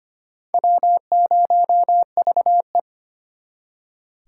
Morse code, heard as a keyed tone, about 31 wpm, 700 Hz. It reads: W0VE